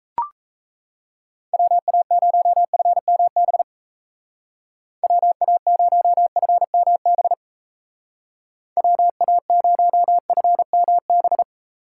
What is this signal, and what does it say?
Morse code, heard as a keyed tone, about 35 wpm, 700 Hz. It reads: WA0FMB WA0FMB WA0FMB